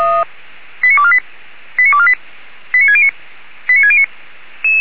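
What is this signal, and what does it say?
Five-tone selective call (ZVEI), heard as quick runs of five tones, tones of about 70 ms. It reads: 89248 89248 89609 89609
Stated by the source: ZVEI five-tone selective call received with an SDR.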